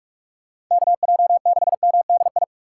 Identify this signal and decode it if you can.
Morse code, heard as a keyed tone, about 42 wpm, 700 Hz. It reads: KJ6MDI
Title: USA Call Signs - 6 Characters - 4x Character Spacing - 45wpm